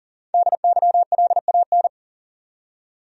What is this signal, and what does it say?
Morse code, heard as a keyed tone, about 40 wpm, 700 Hz. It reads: DYLAN
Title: USA Common Names - Max 5 Letters - 40wpm